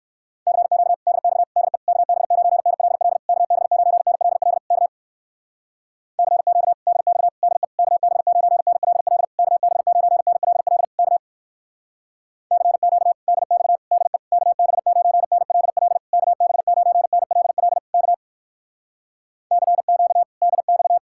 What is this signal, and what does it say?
Morse code, heard as a keyed tone, about 59 wpm, 700 Hz. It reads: CQ DX DE KB9NLL KB9NLL K CQ DX DE KB9NLL KB9NLL K CQ DX DE KB9NLL KB9NLL K CQ DX